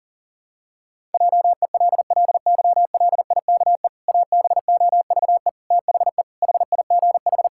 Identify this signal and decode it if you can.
Morse code, heard as a keyed tone, about 40 wpm, 700 Hz. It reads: JELLYLIKE ABOVE THE HIGH
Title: Pangram Challenge v2 - 40wpm